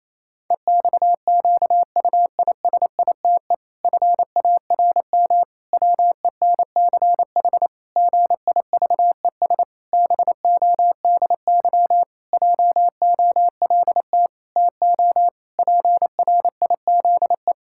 Morse code, heard as a keyed tone, about 28 wpm, 700 Hz. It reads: EXQUISITE FARM WENCH GIVES BODY JOLT TO PRIZE
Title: Pangram Challenge v2 - 28wpm